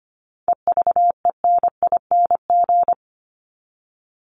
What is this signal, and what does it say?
Morse code, heard as a keyed tone, about 25 wpm, 700 Hz. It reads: EVENING
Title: Top 1501-1750 Words 25wpm